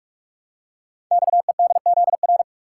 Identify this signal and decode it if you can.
Morse code, heard as a keyed tone, about 44 wpm, 700 Hz. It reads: XEDZR
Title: Code Groups - 5 Letters - 45wpm